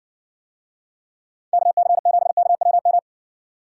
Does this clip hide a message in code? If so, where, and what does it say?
Yes: Morse code from 1.5 to 3.0 s, heard as a keyed tone, about 60 wpm, 700 Hz. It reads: KC7KWM